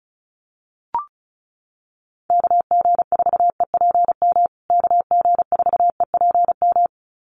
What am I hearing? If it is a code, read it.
Morse code, heard as a keyed tone, about 34 wpm, 700 Hz. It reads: KG4EPM KG4EPM
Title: USA Call Signs Repeated 2x 35wpm